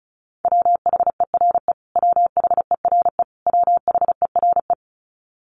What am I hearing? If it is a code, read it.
Morse code, heard as a keyed tone, about 34 wpm, 700 Hz. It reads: WHERE WHERE WHERE
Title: Top 300 Words Repeated 35wpm